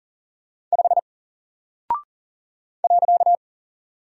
Morse code, heard as a keyed tone, about 40 wpm, 700 Hz. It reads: 5 .
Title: Single Character (Letter:Number:Punctuation:Prosign) Alt 40wpm